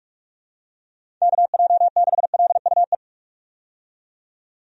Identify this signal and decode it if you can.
Morse code, heard as a keyed tone, about 45 wpm, 700 Hz. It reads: KJ6LUE
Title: USA Call Signs - 6 Characters - 45wpm